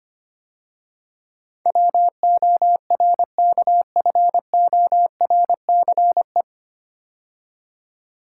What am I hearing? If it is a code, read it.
Morse code, heard as a keyed tone, about 25 wpm, 700 Hz. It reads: WORKFORCE
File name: practice / 25wpm / 9 Letter Words - First 3 Letters Given - 25wpm